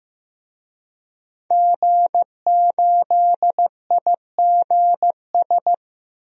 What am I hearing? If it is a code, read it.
Morse code, heard as a keyed tone, about 15 wpm, 700 Hz. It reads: G8IGS
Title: DXCC-100 Call Signs - 5 Characters - 15wpm